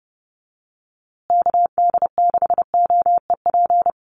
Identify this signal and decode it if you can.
Morse code, heard as a keyed tone, about 30 wpm, 700 Hz. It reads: KD6OEP
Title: USA Call Signs Repeated 2x 30wpm